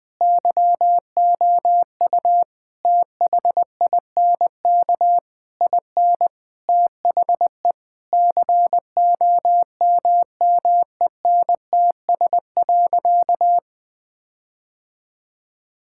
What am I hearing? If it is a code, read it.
Morse code, heard as a keyed tone, about 21 wpm, 700 Hz. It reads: YOU THINK IN THE COMMENTS.
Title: Sentences with 8-Word Phrases - Spoken Beginning - 20wpm